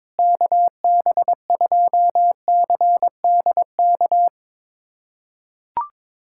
Morse code, heard as a keyed tone, about 22 wpm, 700 Hz. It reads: KB2CDK
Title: USA Callsign DE Callsign 22wpm